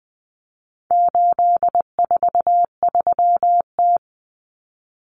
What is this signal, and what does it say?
Morse code, heard as a keyed tone, about 20 wpm, 700 Hz. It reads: 843T
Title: Common Cut Numbers - 4-Digit 20wpm